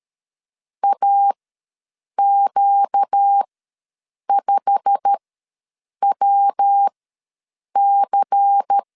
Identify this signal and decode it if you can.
Morse code, heard as a keyed tone, about 13 wpm, spaced as for 7 wpm, 800 Hz. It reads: AQ5WC
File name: LEZIONI / 8 YJ45 / 8 YJ45 ws 12wpm 3146